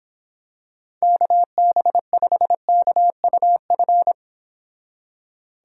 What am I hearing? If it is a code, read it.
Morse code, heard as a keyed tone, about 26 wpm, 700 Hz. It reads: KB5KUF